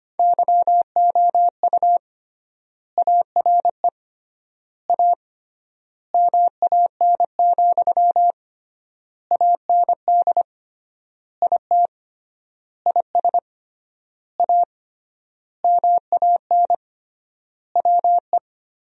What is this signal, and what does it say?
Morse code, heard as a keyed tone, about 25 wpm, 700 Hz. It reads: YOU ARE A MAN, AND IT IS A MAN WE
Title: 3x Word-Spacing - Sentences from Top 200 Words - 25wpm